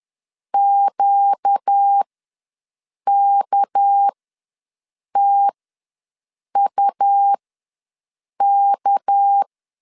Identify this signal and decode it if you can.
Morse code, heard as a keyed tone, about 10 wpm, spaced as for 6 wpm, 800 Hz. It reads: QKTUK
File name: LEZIONI / 6 PKX0 / 6 PK ws 10wpm 3745